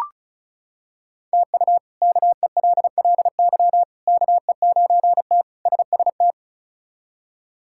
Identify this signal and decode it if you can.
Morse code, heard as a keyed tone, about 34 wpm, 700 Hz. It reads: TU KELLY KE9T SST
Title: K1USN SST QSO Practice - 35wpm